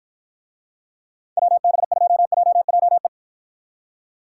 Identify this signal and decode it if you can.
Morse code, heard as a keyed tone, about 52 wpm, 700 Hz. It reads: WB2JJE